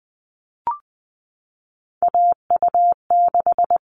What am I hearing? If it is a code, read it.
Morse code, heard as a keyed tone, about 20 wpm, 700 Hz. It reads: AU6
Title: All Cut Numbers - 3-Digit 20wpm